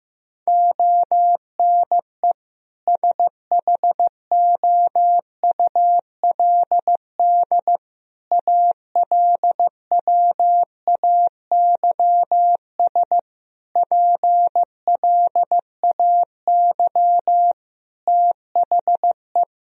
Morse code, heard as a keyed tone, about 14 wpm, 700 Hz. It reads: ONE SHOULD ALWAYS PLAY THE
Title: Sentences from Top 1000 Words - Review Part 3 - 15wpm